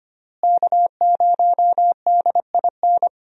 Morse code, heard as a keyed tone, about 25 wpm, 700 Hz. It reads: K0DIN
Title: USA Call Signs - 5 Characters - Repeated 3x - 25wpm